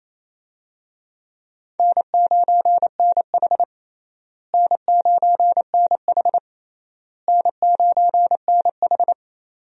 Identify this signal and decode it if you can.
Morse code, heard as a keyed tone, about 28 wpm, 700 Hz. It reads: N9NH N9NH N9NH
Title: USA Call Signs - 4 Characters - Repeated 3x - 28wpm